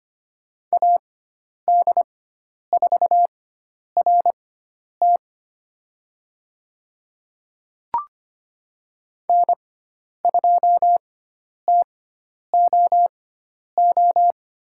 Morse code, heard as a keyed tone, about 25 wpm, spaced as for 10 wpm, 700 Hz. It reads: AD4RT N2TOO
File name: course / 25:10wpm / 095 Lesson 16g - Letter c - Call Signs 25wpm